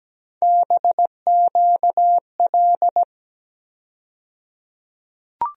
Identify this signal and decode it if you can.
Morse code, heard as a keyed tone, about 17 wpm, 700 Hz. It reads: BQL
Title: USA Call Sign Postfixes - 3 Letters 17wpm